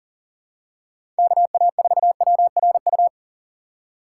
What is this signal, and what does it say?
Morse code, heard as a keyed tone, about 40 wpm, 700 Hz. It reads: KA4WRU
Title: USA Call Signs - 6 Characters - 6x Character Spacing - 40wpm